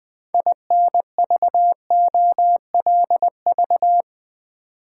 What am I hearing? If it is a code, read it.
Morse code, heard as a keyed tone, about 20 wpm, 700 Hz. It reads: INVOLV
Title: Build A Word - 2 Letters At A Time - 8-Letter Words - 20wpm